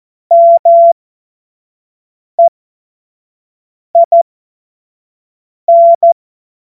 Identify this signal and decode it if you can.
Morse code, heard as a keyed tone, about 14 wpm, spaced as for 5 wpm, 680 Hz. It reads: MEIN
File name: Morsekurs / Witze / Witz 20. Januar 2026 25 F70